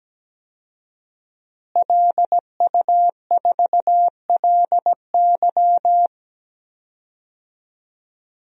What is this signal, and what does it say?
Morse code, heard as a keyed tone, about 17 wpm, 700 Hz. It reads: LU4LY
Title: DXCC-100 Call Signs - 5 Characters - Repeated 3x - 17wpm